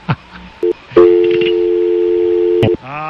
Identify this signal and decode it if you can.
Dial tone (North America), 350 + 440 Hz.